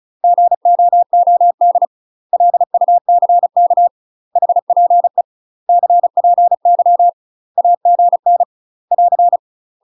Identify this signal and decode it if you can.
Morse code, heard as a keyed tone, about 36 wpm, 700 Hz. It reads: GOOD LUCK HPE CPY AGN <AR>
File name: UK MASTER QSOs TRAINING / 35WPM / UK MASTER QSOs 0041